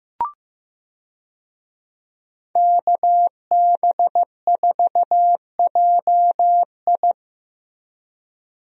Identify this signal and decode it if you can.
Morse code, heard as a keyed tone, about 15 wpm, 700 Hz. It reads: KB4JI
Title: USA Call Signs - 5 Characters - Repeated 2x - 15wpm